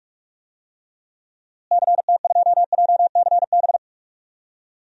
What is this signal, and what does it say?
Morse code, heard as a keyed tone, about 47 wpm, 700 Hz. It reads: CT2JCB
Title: DXCC-100 Call Signs - 6 Characters - Repeated 2x - 45wpm